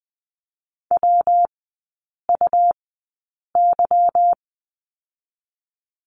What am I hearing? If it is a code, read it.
Morse code, heard as a keyed tone, about 20 wpm, spaced as for 8 wpm, 690 Hz. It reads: WUY
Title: 212 Lesson 34e - Number 8 - Three Characters 20wpm